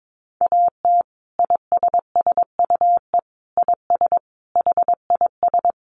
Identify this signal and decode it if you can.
Morse code, heard as a keyed tone, about 22 wpm, 700 Hz. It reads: AT ISSUE IS HIS
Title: Sentences from Top 900 Words 22wpm